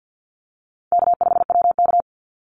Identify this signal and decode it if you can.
Morse code, heard as a keyed tone, about 50 wpm, 700 Hz. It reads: K5WK
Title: Easy Call Signs - 50wpm